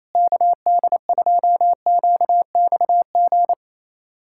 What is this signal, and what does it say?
Morse code, heard as a keyed tone, about 28 wpm, 700 Hz. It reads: KD2QXG
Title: USA Callsign DE Callsign - 6 Characters - 28wpm